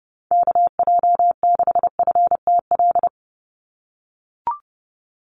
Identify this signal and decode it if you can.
Morse code, heard as a keyed tone, about 30 wpm, 700 Hz. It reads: KJ6FTL
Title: USA Call Signs 30wpm